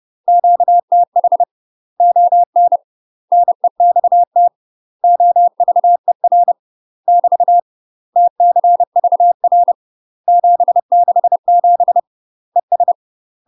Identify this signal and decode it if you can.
Morse code, heard as a keyed tone, about 30 wpm, 700 Hz. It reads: QTH ON NEXT OVER <BT> TCVR 767 ES
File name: UK MASTER QSOs TRAINING / 30WPM / UK MASTER QSOs 0018